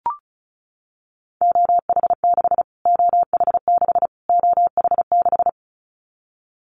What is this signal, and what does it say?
Morse code, heard as a keyed tone, about 34 wpm, 700 Hz. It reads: OH6 OH6 OH6
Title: Callsign Prefixes for Top 25 DXCC - Repeated 3x - 35wpm